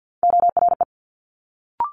Morse code, heard as a keyed tone, about 50 wpm, 700 Hz. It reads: ORE
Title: Multi-letter Phonemes Quick Fire 50wpm